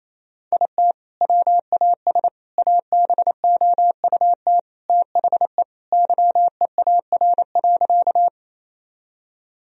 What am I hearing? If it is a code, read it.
Morse code, heard as a keyed tone, about 28 wpm, 700 Hz. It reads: IT WAS ABOUT THE YEAR.
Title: Sentences from Top 1000 Words - Review Part 1 - 28wpm